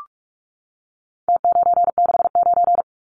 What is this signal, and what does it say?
Morse code, heard as a keyed tone, about 44 wpm, 700 Hz. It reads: T969
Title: Numbers - Contest Style - 4-Digit 45wpm